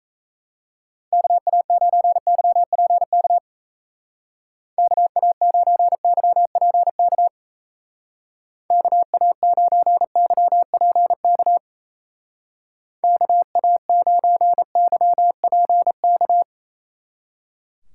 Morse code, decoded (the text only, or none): KA9YPK KA9YPK KA9YPK KA9YPK